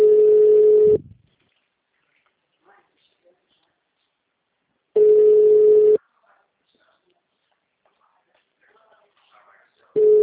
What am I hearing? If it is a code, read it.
Ringback tone (Europe, 425 Hz), with speech.